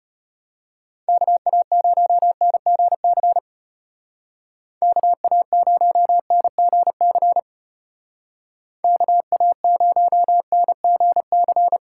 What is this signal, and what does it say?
Morse code, heard as a keyed tone, about 30 wpm, 700 Hz. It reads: KA0NGC KA0NGC KA0NGC
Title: Speed Racing - USA Call Signs 25wpm